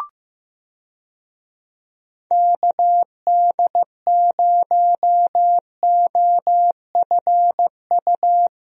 Morse code, heard as a keyed tone, about 15 wpm, 700 Hz. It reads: KD0OFU
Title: USA Call Signs - 6 Characters - Repeated 2x - 15wpm